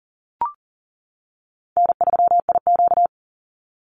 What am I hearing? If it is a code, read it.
Morse code, heard as a keyed tone, about 40 wpm, 700 Hz. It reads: N3IQ
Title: Call Signs - 40wpm